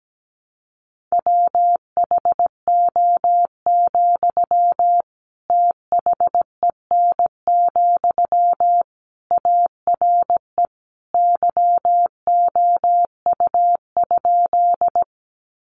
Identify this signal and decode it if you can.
Morse code, heard as a keyed tone, about 17 wpm, 700 Hz. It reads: WHO, THEN, ARE YOU?
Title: Sentences from Top 100 Words 17wpm